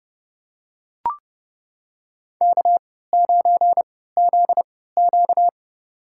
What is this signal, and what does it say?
Morse code, heard as a keyed tone, about 30 wpm, spaced as for 17 wpm, 700 Hz. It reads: K9ZQ
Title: USA Call Signs - 4 Characters - 3x Character Spacing - 30wpm